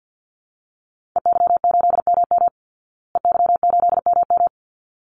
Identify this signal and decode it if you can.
Morse code, heard as a keyed tone, about 49 wpm, 700 Hz. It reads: EY8MM EY8MM
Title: Call Signs Repeated 3x - 50wpm